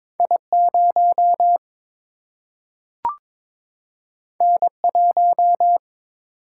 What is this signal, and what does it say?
Morse code, heard as a keyed tone, about 22 wpm, 700 Hz. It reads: I0 N1
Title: Single-Letter Single-Number 22wpm